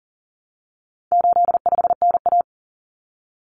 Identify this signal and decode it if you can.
Morse code, heard as a keyed tone, about 40 wpm, 700 Hz. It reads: 85NA
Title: Common Cut Numbers - 4-Digit 40wpm